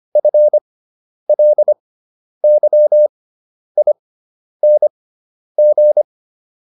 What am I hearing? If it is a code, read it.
Morse code, heard as a keyed tone, about 25 wpm, spaced as for 10 wpm, 600 Hz. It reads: FLYING